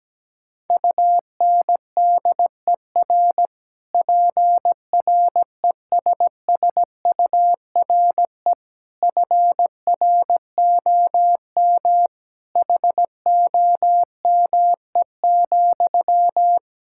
Morse code, heard as a keyed tone, about 17 wpm, 700 Hz. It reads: UNDER PRESSURE FROM HOME,
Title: Sentences from Top 1000 Words - Review Part 6 - 17wpm